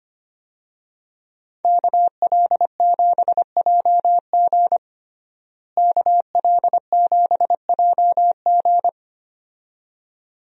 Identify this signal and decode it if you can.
Morse code, heard as a keyed tone, about 25 wpm, 700 Hz. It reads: KL7JG KL7JG
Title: DXCC-100 Call Signs - 5 Characters - Repeated 2x - 25wpm